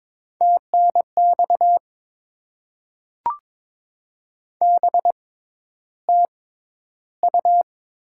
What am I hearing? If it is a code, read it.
Morse code, heard as a keyed tone, about 21 wpm, spaced as for 11 wpm, 690 Hz. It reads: TNX BTU
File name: Top 20 QSO Elements - 6x Character Spacing - 22wpm